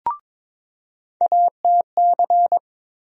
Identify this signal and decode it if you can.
Morse code, heard as a keyed tone, about 22 wpm, 700 Hz. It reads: ATC
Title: USA Call Sign Postfixes - 3 Letters 22wpm